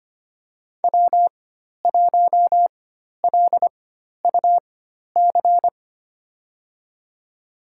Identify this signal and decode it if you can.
Morse code, heard as a keyed tone, about 25 wpm, spaced as for 12 wpm, 700 Hz. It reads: W1LUC
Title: USA Call Signs - 5 Characters - 4x Character Spacing - 25wpm